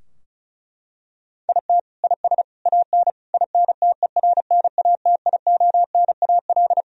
Morse code, heard as a keyed tone, about 36 wpm, 700 Hz. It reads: IT IS AN INTERNATIONAL
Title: Sentences from Top 1000 Words - Review Part 4 - 35wpm